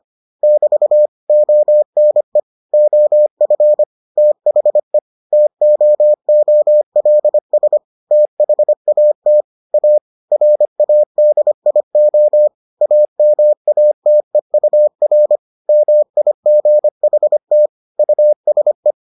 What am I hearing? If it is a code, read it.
Morse code, heard as a keyed tone, about 25 wpm, 600 Hz. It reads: <BT> ONE OF THE TOOLS THAT A RADIO AMATEUR MIGHT USE